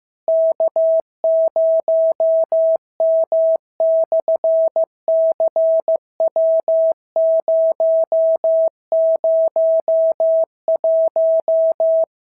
Morse code, heard as a keyed tone, about 15 wpm, 650 Hz. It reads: K0M/CW001
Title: SOTA QSO Practice 15wpm